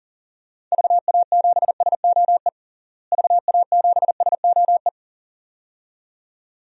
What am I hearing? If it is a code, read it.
Morse code, heard as a keyed tone, about 40 wpm, 700 Hz. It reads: VA7SOE VA7SOE
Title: DXCC-100 Call Signs - 6 Characters - Repeated 2x - 40wpm